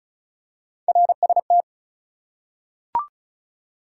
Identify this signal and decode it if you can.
Morse code, heard as a keyed tone, about 34 wpm, 700 Hz. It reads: RST
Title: Top 180 QSO Elements 35wpm